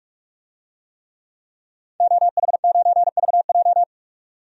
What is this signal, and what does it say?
Morse code, heard as a keyed tone, about 46 wpm, 700 Hz. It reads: OH9VJ
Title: DXCC-100 Call Signs - 5 Characters - 45wpm